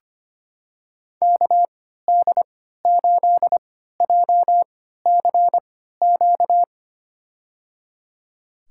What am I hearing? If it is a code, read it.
Morse code, heard as a keyed tone, about 25 wpm, spaced as for 14 wpm, 700 Hz. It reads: KD8JCQ
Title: USA Call Signs - 6 Characters - 3x Character Spacing - 25wpm